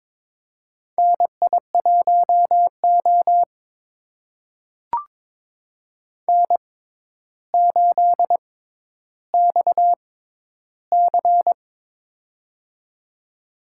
Morse code, heard as a keyed tone, about 23 wpm, spaced as for 11 wpm, 700 Hz. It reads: NI1O N8XC
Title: USA Call Signs - 4 Characters - 6x Character Spacing - 22wpm